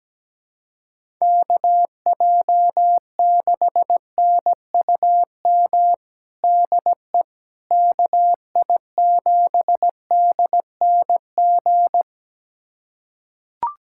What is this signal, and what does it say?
Morse code, heard as a keyed tone, about 17 wpm, 700 Hz. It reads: KJ6NUM DE KI7DNG
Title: USA Callsign DE Callsign - 6 Characters - 17wpm